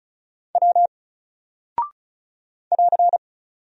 Morse code, heard as a keyed tone, about 36 wpm, 700 Hz. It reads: W <AR>
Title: Single Character (Letter:Number:Punctuation:Prosign) 35wpm